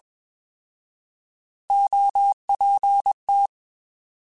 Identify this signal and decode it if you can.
Morse code, heard as a keyed tone, about 21 wpm, 800 Hz. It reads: OPT
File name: Alphabetical / Alphabetical mp3s / Os @20